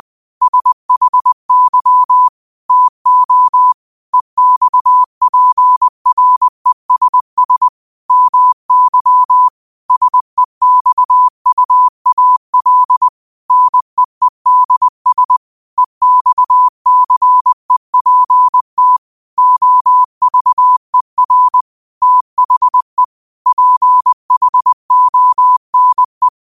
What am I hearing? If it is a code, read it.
Morse code, heard as a keyed tone, about 20 wpm, 1000 Hz. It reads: SHY TO EXPRESS MY SEXUAL NEEDS EXCEPT OVER THE PHONE